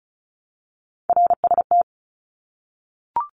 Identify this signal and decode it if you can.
Morse code, heard as a keyed tone, about 34 wpm, 700 Hz. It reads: RST
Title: Abbreviations Repeated 3x - 35wpm